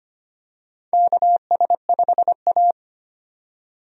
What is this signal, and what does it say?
Morse code, heard as a keyed tone, about 25 wpm, 700 Hz. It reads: KS5A
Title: USA Call Signs - 4 Characters - 25wpm